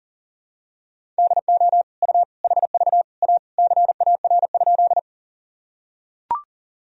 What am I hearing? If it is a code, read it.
Morse code, heard as a keyed tone, about 37 wpm, 700 Hz. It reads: DO U HV A CAR?
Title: Sentences from Top 300 Words and Abbreviations 40wpm